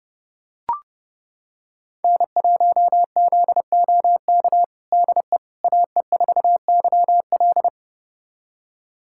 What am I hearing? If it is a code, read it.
Morse code, heard as a keyed tone, about 30 wpm, 700 Hz. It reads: N1ZOK DE AE4YL
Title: USA Callsign DE Callsign - 5 Characters - 30wpm